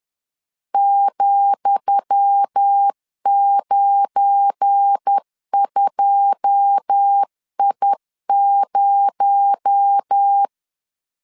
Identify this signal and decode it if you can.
Morse code, heard as a keyed tone, about 10 wpm, 800 Hz. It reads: ,92I0
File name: LEZIONI / 10 SIGNS / 10D COMA 10wpm 3013